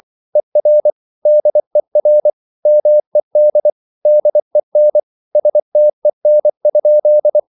Morse code, heard as a keyed tone, about 24 wpm, 600 Hz. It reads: ER DER MED DEN STEN?
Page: Ferie 24 wpm